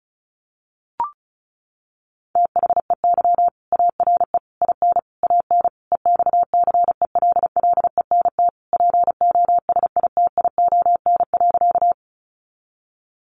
Morse code, heard as a keyed tone, about 34 wpm, 700 Hz. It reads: THEY ARE IN AN EXCELLENT POSITION.
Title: Sentences from Top 1000 Words 35wpm